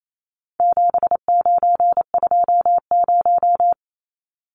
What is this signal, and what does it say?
Morse code, heard as a keyed tone, about 28 wpm, 700 Hz. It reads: 7920
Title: Numbers - 4-Digit 28wpm